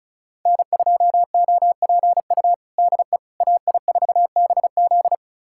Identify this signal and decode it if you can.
Morse code, heard as a keyed tone, about 36 wpm, 700 Hz. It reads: N2OPU DE AI4BZ
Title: USA Callsign DE Callsign 35wpm